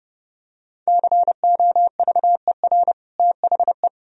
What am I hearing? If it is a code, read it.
Morse code, heard as a keyed tone, about 30 wpm, 700 Hz. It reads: COVER THE
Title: Sentences from Top 1000 Words - Review Part 1 - 30wpm